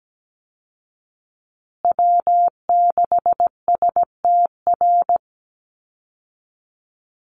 Morse code, heard as a keyed tone, about 17 wpm, 700 Hz. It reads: W6STR
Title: Call Signs Repeated 3x - 17wpm